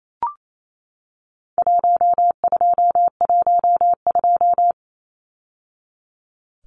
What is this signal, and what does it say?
Morse code, heard as a keyed tone, about 28 wpm, 700 Hz. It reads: 1212
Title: Numbers - Contest Style - 4-Digit 28wpm